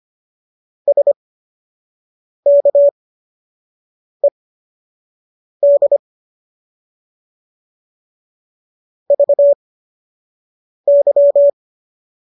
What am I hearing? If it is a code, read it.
Morse code, heard as a keyed tone, about 25 wpm, spaced as for 6 wpm, 570 Hz. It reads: SKED VY